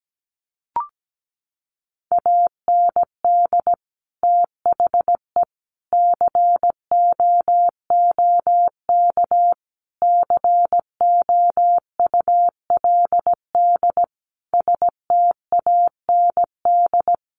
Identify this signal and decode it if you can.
Morse code, heard as a keyed tone, about 17 wpm, 700 Hz. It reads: AND THE COOK COULD STAND
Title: Sentences from Top 1300 Words 17wpm